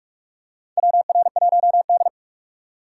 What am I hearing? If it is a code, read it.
Morse code, heard as a keyed tone, about 41 wpm, 700 Hz. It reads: WR1D
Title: USA Call Signs - 4x Character Spacing - 45wpm